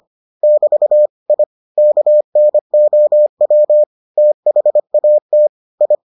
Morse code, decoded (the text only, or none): <BT> I KNOW THAT I